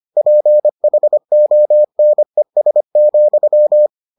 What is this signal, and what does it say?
Morse code, heard as a keyed tone, about 25 wpm, 600 Hz. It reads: PHONES,